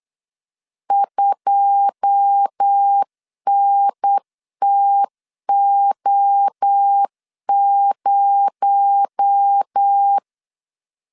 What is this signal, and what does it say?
Morse code, heard as a keyed tone, about 8 wpm, 800 Hz. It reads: 2NTO0
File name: LEZIONI / 10 SIGNS / 10B INT 8wpm 3534